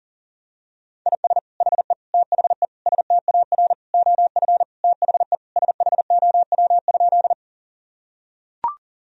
Morse code, heard as a keyed tone, about 40 wpm, 700 Hz. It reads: IS HE THE STAR OF THE SHOW?
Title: Sentences from Top 1000 Words - Review Part 2 - 40wpm